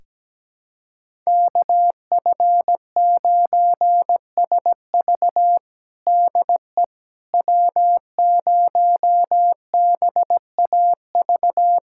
Morse code, heard as a keyed tone, about 17 wpm, 700 Hz. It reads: KF9SV DE W0BAV
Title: USA Callsign DE Callsign - 5 Characters - 17wpm